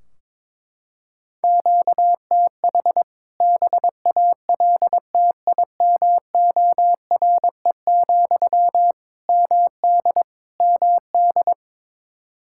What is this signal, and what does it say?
Morse code, heard as a keyed tone, about 22 wpm, 700 Hz. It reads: QTH BALTIMORE, MD MD
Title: K1USN SST QSO Practice 22wpm